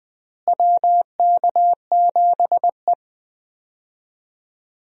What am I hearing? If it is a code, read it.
Morse code, heard as a keyed tone, about 20 wpm, 700 Hz. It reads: WK7E
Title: USA Call Signs - 4 Characters - Repeated 3x - 20wpm